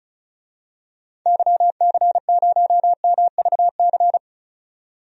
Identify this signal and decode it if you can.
Morse code, heard as a keyed tone, about 36 wpm, 700 Hz. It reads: YC0MVC
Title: DXCC-100 Call Signs - 6 Characters - 35wpm